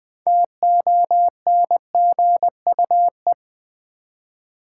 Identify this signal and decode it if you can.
Morse code, heard as a keyed tone, about 20 wpm, 700 Hz. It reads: TONGUE
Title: Six Letter Words - 20wpm